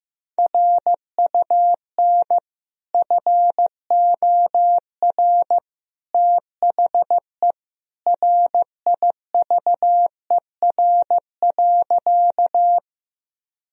Morse code, heard as a keyed tone, about 15 wpm, 700 Hz. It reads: RUN FOR THE RIVER.
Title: Sentences from Top 1000 Words - Review Part 1 - 15wpm